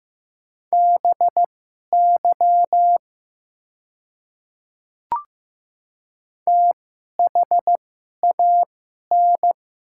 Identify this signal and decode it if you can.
Morse code, heard as a keyed tone, about 15 wpm, spaced as for 11 wpm, 700 Hz. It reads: BY THAN
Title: 2x Character Spacing - Top 300 Words - 15wpm